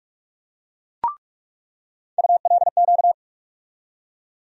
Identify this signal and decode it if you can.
Morse code, heard as a keyed tone, about 46 wpm, 700 Hz. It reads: ULQ
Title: USA Call Sign Postfixes - 3 Letters 45wpm